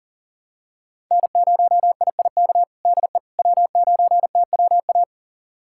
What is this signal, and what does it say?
Morse code, heard as a keyed tone, about 37 wpm, 700 Hz. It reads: N0IIK DE W9TWA
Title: USA Callsign DE Callsign - 5 Characters - 40wpm